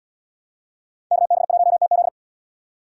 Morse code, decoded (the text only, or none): KB1TZ